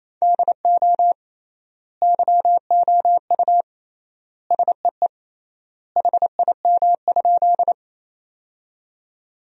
Morse code, heard as a keyed tone, about 27 wpm, 700 Hz. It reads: DO YOU SEE HIM?
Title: 3x Word-Spacing - Sentences from Top 100 Words - 28wpm